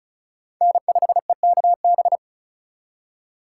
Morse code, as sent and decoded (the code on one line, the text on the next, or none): -. ..... -.- -...
N5EKB